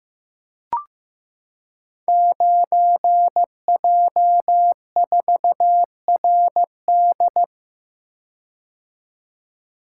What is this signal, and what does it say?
Morse code, heard as a keyed tone, about 15 wpm, 700 Hz. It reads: 9J4RD
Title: Code Groups - 5 Letter:Numbers - 15wpm